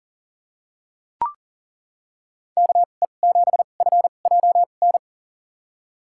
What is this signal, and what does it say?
Morse code, heard as a keyed tone, about 40 wpm, spaced as for 29 wpm, 700 Hz. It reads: KE7FJN
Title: USA Call Signs - 6 Characters - 2x Character Spacing - 40wpm